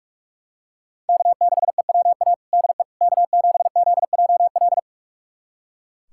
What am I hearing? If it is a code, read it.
Morse code, heard as a keyed tone, about 42 wpm, 700 Hz. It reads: K6EWA DE K7ZJL
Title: USA Callsign DE Callsign - 5 Characters - 45wpm